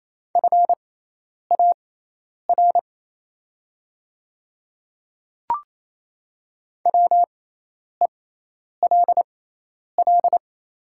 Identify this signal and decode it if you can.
Morse code, heard as a keyed tone, about 27 wpm, spaced as for 10 wpm, 700 Hz. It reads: FAR WELL
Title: 6x Character Spacing - Top 300 Words - 28wpm